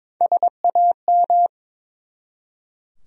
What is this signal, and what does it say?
Morse code, heard as a keyed tone, about 21 wpm, 700 Hz. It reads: SAM
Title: USA Common Names - Max 4 Letters - 22wpm